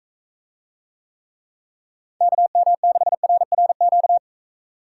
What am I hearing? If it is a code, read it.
Morse code, heard as a keyed tone, about 41 wpm, 700 Hz. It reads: KM6RRQ